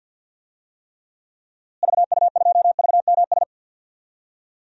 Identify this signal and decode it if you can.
Morse code, heard as a keyed tone, about 51 wpm, 700 Hz. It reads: VU2VMS
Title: DXCC-100 Call Signs - 6 Characters - Repeated 2x - 50wpm